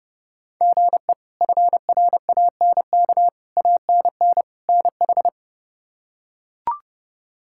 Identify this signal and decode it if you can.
Morse code, heard as a keyed tone, about 32 wpm, 700 Hz. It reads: GE FRANK ANN NH
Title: K1USN SST QSO Practice - 30wpm